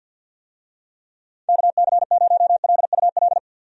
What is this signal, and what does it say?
Morse code, heard as a keyed tone, about 52 wpm, 700 Hz. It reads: KC0LUL
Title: USA Call Signs - 6 Characters - 50wpm